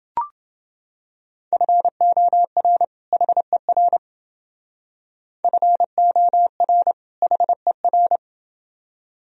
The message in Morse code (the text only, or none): FOR HER FOR HER